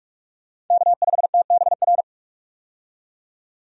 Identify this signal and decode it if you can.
Morse code, heard as a keyed tone, about 46 wpm, 700 Hz. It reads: K5TBR
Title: USA Call Signs - 5 Characters - 45wpm